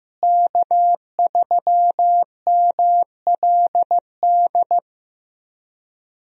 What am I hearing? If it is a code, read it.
Morse code, heard as a keyed tone, about 15 wpm, 700 Hz. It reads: K3MLD
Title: USA Call Signs - 5 Characters - Repeated 2x - 15wpm